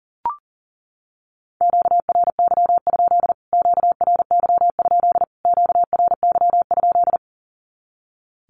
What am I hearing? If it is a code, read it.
Morse code, heard as a keyed tone, about 40 wpm, 700 Hz. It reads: QRY? QRY? QRY?
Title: Q-Codes Repeated 3x 40wpm